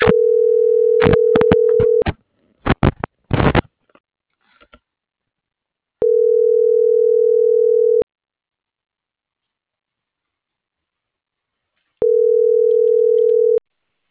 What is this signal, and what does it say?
Ringback tone (North America), 440 + 480 Hz.